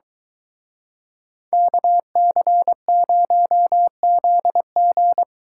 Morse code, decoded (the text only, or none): KC0ZG